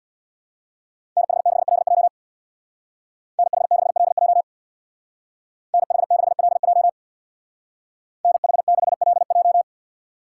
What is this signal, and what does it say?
Morse code, heard as a keyed tone, about 62 wpm, 700 Hz. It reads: NH6LJ NH6LJ NH6LJ NH6LJ
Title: Speed Racing - USA 2x2 Call Signs 50wpm